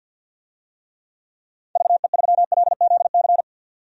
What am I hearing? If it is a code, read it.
Morse code, heard as a keyed tone, about 53 wpm, 700 Hz. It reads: VE3LZC
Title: DXCC-100 Call Signs - 6 Characters - 50wpm